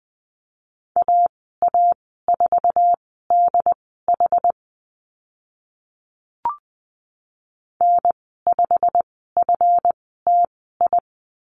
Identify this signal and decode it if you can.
Morse code, heard as a keyed tone, about 20 wpm, spaced as for 14 wpm, 700 Hz. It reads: AA4DH N5FTI
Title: 140 Lesson 23g - Letter f - Call Signs 20wpm